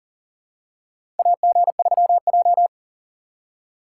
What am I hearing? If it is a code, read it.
Morse code, heard as a keyed tone, about 40 wpm, 700 Hz. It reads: AG3J
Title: USA Call Signs - 4 Characters - 2x Character Spacing - 40wpm